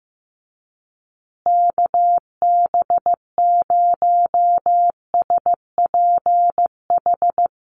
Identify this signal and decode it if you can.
Morse code, heard as a keyed tone, about 15 wpm, 700 Hz. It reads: KB0SPH
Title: USA Call Signs Repeated 2x 15wpm